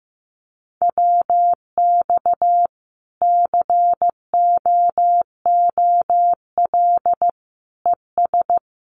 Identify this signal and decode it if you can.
Morse code, heard as a keyed tone, about 15 wpm, 700 Hz. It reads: WX COOL ES
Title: Common QSO Phrases 15wpm